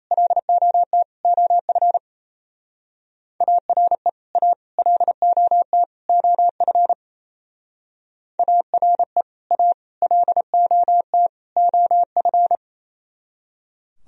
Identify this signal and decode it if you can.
Morse code, heard as a keyed tone, about 35 wpm, 700 Hz. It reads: LOT OF ARE A LOT OF ARE A LOT OF
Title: Speed-Racing - Sets of 4 Words 28wpm